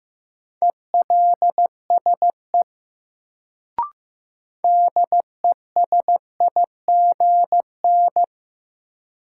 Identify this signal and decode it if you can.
Morse code, heard as a keyed tone, about 15 wpm, 700 Hz. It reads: ELSE DESIGN
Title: Sentences from Top 800 Words - New Word Drill - 15wpm